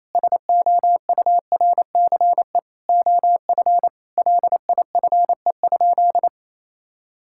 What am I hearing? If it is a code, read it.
Morse code, heard as a keyed tone, about 28 wpm, 700 Hz. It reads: SOURCE OF LIFE?
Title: Sentences from Top 1000 Words - Review Part 5 - 28wpm